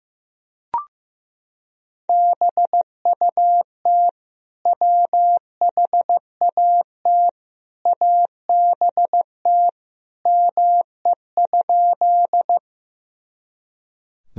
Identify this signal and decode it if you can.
Morse code, heard as a keyed tone, about 15 wpm, 700 Hz. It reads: BUT WHAT ABT ME?
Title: Sentences from Top 200 Words and Abbreviations 15wpm